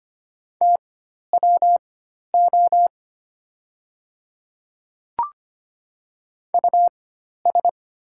Morse code, heard as a keyed tone, about 25 wpm, spaced as for 12 wpm, 690 Hz. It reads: TWO US
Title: 4x Character Spacing - Top 200 Words - 25wpm